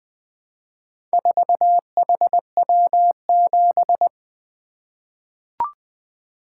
Morse code, heard as a keyed tone, about 20 wpm, 700 Hz. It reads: 4HW7
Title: Four Letter-Number Combinations - 20wpm